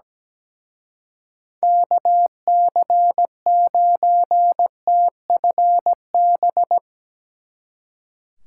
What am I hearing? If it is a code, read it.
Morse code, heard as a keyed tone, about 19 wpm, 700 Hz. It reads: KC9TFB
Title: Speed Racing - USA Call Signs 17wpm